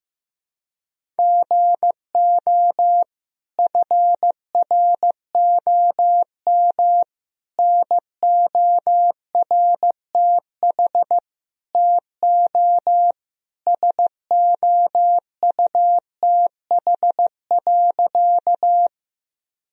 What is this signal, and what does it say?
Morse code, heard as a keyed tone, about 15 wpm, 700 Hz. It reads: GO FROM NORTH TO SOUTH.
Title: Sentences from Top 1000 Words - Review Part 2 - 15wpm